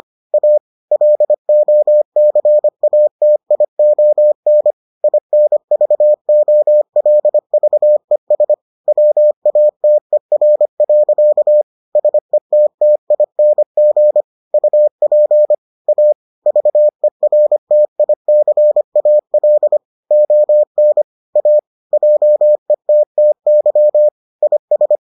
Morse code, heard as a keyed tone, about 25 wpm, 600 Hz. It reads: A LOCATION INVOLVES WATER. SETTING UP A VERTICAL ON A JETTY IS